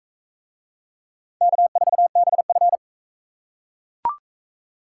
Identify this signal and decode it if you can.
Morse code, heard as a keyed tone, about 41 wpm, 700 Hz. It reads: K4BF